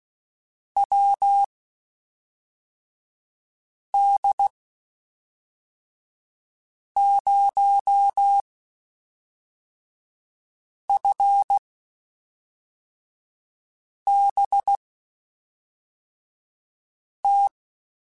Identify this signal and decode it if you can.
Morse code, heard as a keyed tone, about 16 wpm, spaced as for 3 wpm, 800 Hz. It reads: WD0FBT
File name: (spaced) / QSOs Spaced-mp3s / WD0FBT 1 @5 sp